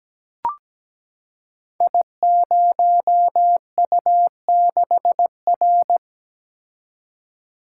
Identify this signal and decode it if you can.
Morse code, heard as a keyed tone, about 17 wpm, 700 Hz. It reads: I0U6R
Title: Code Groups - 5 Letter:Numbers - 17wpm